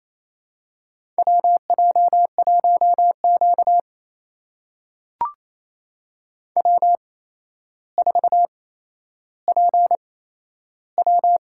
Morse code, heard as a keyed tone, about 28 wpm, spaced as for 12 wpm, 700 Hz. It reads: WJ1Q W4PW
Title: USA Call Signs - 8x Character Spacing - 28wpm